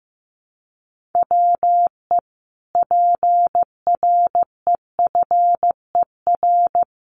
Morse code, heard as a keyed tone, about 15 wpm, 700 Hz. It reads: WE PREFER
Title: Sentences from Top 1100 Words 15wpm